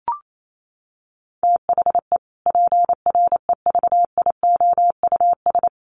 Morse code, heard as a keyed tone, about 28 wpm, 700 Hz. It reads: THE PREVIOUS
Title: Sentences from Top 1500 Words 28wpm